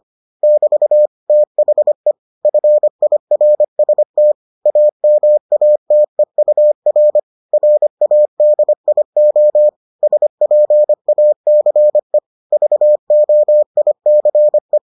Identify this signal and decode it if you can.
Morse code, heard as a keyed tone, about 25 wpm, 600 Hz. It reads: <BT> THE FIRST AMATEUR RADIO SPACE VOICE